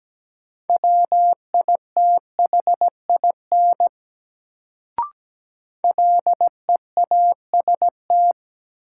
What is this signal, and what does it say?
Morse code, heard as a keyed tone, about 17 wpm, 700 Hz. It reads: WITHIN LEAST
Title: Sentences from Top 700 Words - New Word Drill - 17wpm